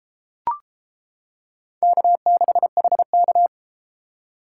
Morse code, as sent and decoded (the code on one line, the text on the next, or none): -.- -.... .... -.-
K6HK